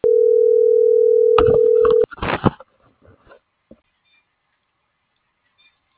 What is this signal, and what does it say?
Ringback tone (North America), 440 + 480 Hz.